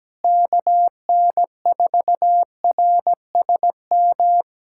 Morse code, heard as a keyed tone, about 17 wpm, 700 Hz. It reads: KN4RSM
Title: USA Callsign DE Callsign 17wpm